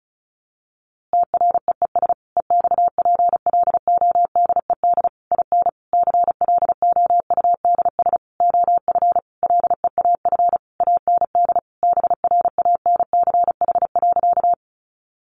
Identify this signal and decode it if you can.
Morse code, heard as a keyed tone, about 34 wpm, 700 Hz. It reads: TREES EXPLODED IN CLOUDS OF LEAF AND BRANCH.
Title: A Sound Of Thunder 35wpm